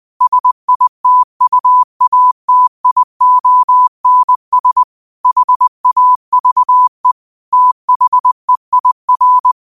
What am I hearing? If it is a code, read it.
Morse code, heard as a keyed tone, about 20 wpm, 1000 Hz. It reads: SITUATIONS HAVE THEIR
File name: qotdfiles / 20wpm